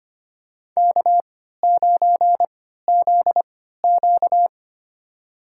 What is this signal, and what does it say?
Morse code, heard as a keyed tone, about 25 wpm, spaced as for 14 wpm, 700 Hz. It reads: K9ZQ
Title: USA Call Signs - 4 Characters - 3x Character Spacing - 25wpm